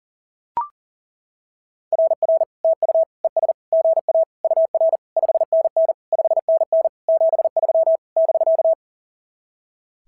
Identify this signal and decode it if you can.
Morse code, heard as a keyed tone, about 37 wpm, 660 Hz. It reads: RR TU ES GA UR 5NN 5NN 73 <BK>
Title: SOTA QSO Practice 40wpm